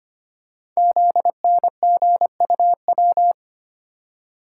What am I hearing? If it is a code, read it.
Morse code, heard as a keyed tone, about 25 wpm, 700 Hz. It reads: ZNGUW
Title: Code Groups - 5 Letters - 25wpm